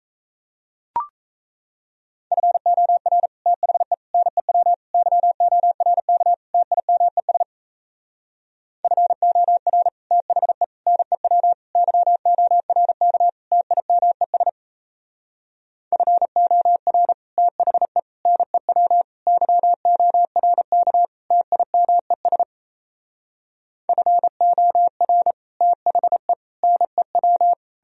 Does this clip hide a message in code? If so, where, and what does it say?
Yes: Morse code from 2.3 to 27.5 s, heard as a keyed tone, about 37 wpm, 700 Hz. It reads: FOR THE NEW YORK TIMES FOR THE NEW YORK TIMES FOR THE NEW YORK TIMES FOR THE NEW